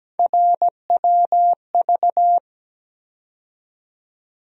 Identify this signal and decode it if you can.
Morse code, heard as a keyed tone, about 17 wpm, 700 Hz. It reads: RWV
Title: USA Call Sign Postfixes - 3 Letters 17wpm